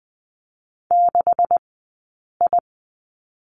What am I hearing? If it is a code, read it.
Morse code, heard as a keyed tone, about 21 wpm, spaced as for 8 wpm, 700 Hz. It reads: 6I
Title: 126 Lesson 21d - Number 6 - Two Characters 20wpm